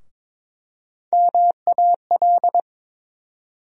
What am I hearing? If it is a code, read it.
Morse code, heard as a keyed tone, about 22 wpm, 700 Hz. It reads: MAL
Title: USA Call Sign Postfixes - 3 Letters 22wpm